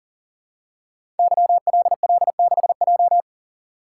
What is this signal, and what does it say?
Morse code, heard as a keyed tone, about 44 wpm, 700 Hz. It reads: YLL6J
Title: Code Groups - 5 Letter:Numbers - 40wpm